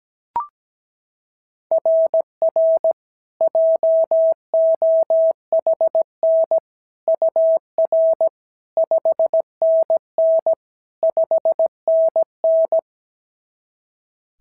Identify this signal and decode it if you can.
Morse code, heard as a keyed tone, about 17 wpm, 690 Hz. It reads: RR JOHN UR 5NN 5NN
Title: SOTA QSO Practice 17wpm